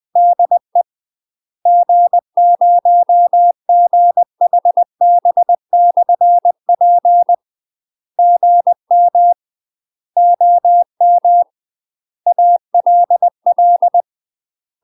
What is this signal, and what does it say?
Morse code, heard as a keyed tone, about 20 wpm, 700 Hz. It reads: DE G0GHB/P GM OM ALL
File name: UK MASTER QSOs TRAINING / 20WPM / UK MASTER QSOs 0055